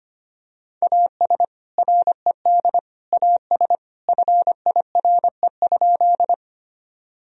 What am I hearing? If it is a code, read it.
Morse code, heard as a keyed tone, about 25 wpm, 700 Hz. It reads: AS RED AS FIRE?
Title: Sentences from Top 1000 Words - Review Part 2 - 25wpm